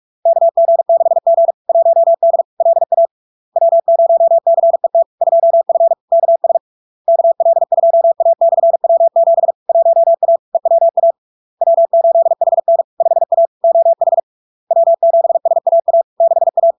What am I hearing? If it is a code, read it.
Morse code, heard as a keyed tone, about 46 wpm, 670 Hz. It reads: KG6G 1D LA W0CET 2F KS KL2A/W7 1A EWA W8HN 5A OH W7SAA 6A
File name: LARK-FD-2022-Log-in-CW-45wpm-670-Hz